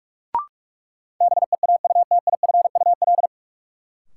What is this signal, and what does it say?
Morse code, heard as a keyed tone, about 46 wpm, 700 Hz. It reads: BEAUTIFUL
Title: Sentences from Top 900 Words - New Word Drill - 45wpm